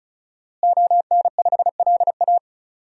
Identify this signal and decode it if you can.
Morse code, heard as a keyed tone, about 34 wpm, 700 Hz. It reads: ON5LA
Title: DXCC-100 Call Signs - 5 Characters - Repeated 3x - 35wpm